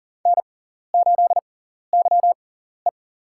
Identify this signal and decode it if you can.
Morse code, heard as a keyed tone, about 40 wpm, spaced as for 14 wpm, 700 Hz. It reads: N8YE
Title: USA Call Signs - 4 Characters - 6x Character Spacing - 40wpm